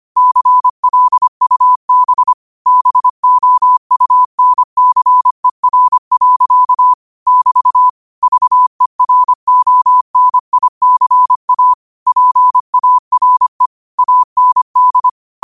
Morse code, heard as a keyed tone, about 25 wpm, 1000 Hz. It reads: CLUB BOUNCER. <BT> VERONICA PARE AND